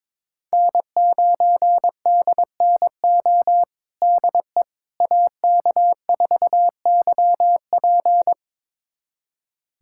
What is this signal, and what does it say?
Morse code, heard as a keyed tone, about 22 wpm, 700 Hz. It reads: N9DNO DE AK4YP
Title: USA Callsign DE Callsign - 5 Characters - 22wpm